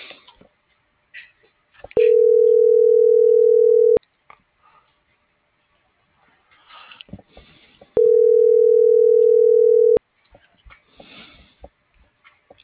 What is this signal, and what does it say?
Ringback tone (North America), 440 + 480 Hz.